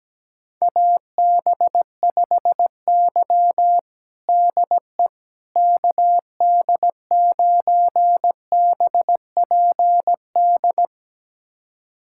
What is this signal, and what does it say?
Morse code, heard as a keyed tone, about 17 wpm, 700 Hz. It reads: AB5Y DE KD9BPD
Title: USA Callsign DE Callsign 17wpm